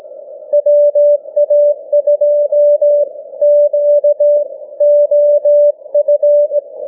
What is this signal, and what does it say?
Morse code, heard as a keyed tone, about 18 wpm, spaced as for 15 wpm, 590 Hz. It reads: WA2QOF